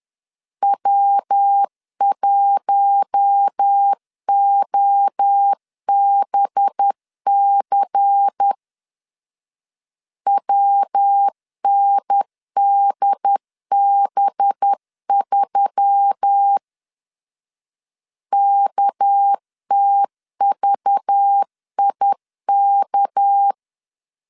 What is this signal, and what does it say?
Morse code, heard as a keyed tone, about 10 wpm, 800 Hz. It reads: W1OBC WNDB3 KTVIK
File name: LEZIONI / 6 PKX0 / 6 PK 10wpm 2436